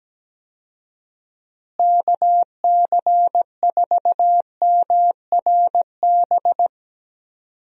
Morse code, heard as a keyed tone, about 17 wpm, 700 Hz. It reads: KC4MRB
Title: USA Call Signs - 6 Characters - Repeated 2x - 17wpm